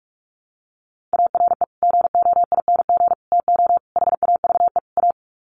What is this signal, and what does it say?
Morse code, heard as a keyed tone, about 44 wpm, 700 Hz. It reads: ARE GOING TO HAVE A
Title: Sets of 5 Words Encore2 45wpm